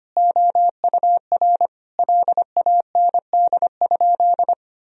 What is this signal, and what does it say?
Morse code, heard as a keyed tone, about 25 wpm, 700 Hz. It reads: OUR LAND?
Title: Sentences from Top 1000 Words - Review Part 4 - 25wpm